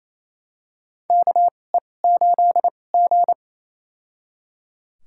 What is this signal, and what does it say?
Morse code, heard as a keyed tone, about 27 wpm, spaced as for 20 wpm, 700 Hz. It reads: KE8G
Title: USA Call Signs - 4 Characters - 2x Character Spacing - 28wpm